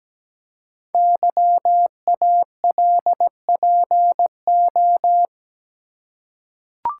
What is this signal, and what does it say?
Morse code, heard as a keyed tone, about 17 wpm, 700 Hz. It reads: YALPO
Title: Code Groups - 5 Letters - 17wpm